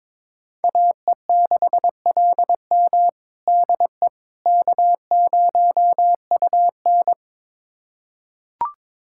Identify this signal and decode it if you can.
Morse code, heard as a keyed tone, about 22 wpm, 700 Hz. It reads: AE6LM DE K0UN
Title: USA Callsign DE Callsign 22wpm